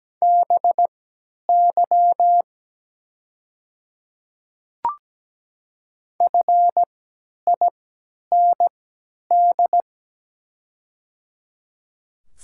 Morse code, heard as a keyed tone, about 17 wpm, spaced as for 10 wpm, 700 Hz. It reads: BY FIND
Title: 3x Character Spacing - Top 300 Words - 17wpm